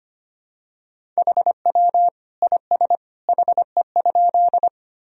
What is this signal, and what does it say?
Morse code, heard as a keyed tone, about 25 wpm, 700 Hz. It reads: HW IS HE?
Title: Sentences from Top 100 Words and Abbreviations 25wpm